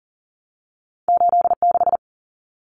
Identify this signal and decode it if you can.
Morse code, heard as a keyed tone, about 40 wpm, 700 Hz. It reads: 86
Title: Common Cut Numbers - 2-Digit 40wpm